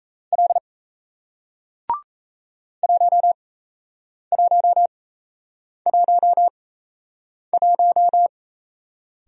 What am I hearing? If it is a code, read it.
Morse code, heard as a keyed tone, about 38 wpm, spaced as for 8 wpm, 700 Hz. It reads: L 1111